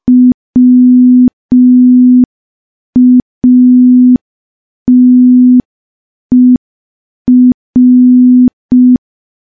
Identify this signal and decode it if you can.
Morse code, heard as a keyed tone, about 5 wpm, 250 Hz. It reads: WATER
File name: morse / morse-training-05WPM